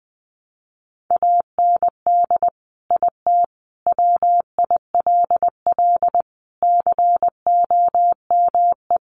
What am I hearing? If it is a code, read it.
Morse code, heard as a keyed tone, about 20 wpm, 700 Hz. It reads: AND IT WILL COME
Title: Sentences from Top 1100 Words 20wpm